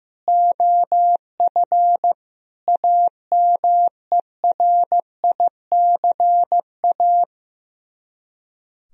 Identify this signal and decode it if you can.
Morse code, heard as a keyed tone, about 15 wpm, 700 Hz. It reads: OF AMERICA
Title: Speed-Racing - Sets of 5 Words 15wpm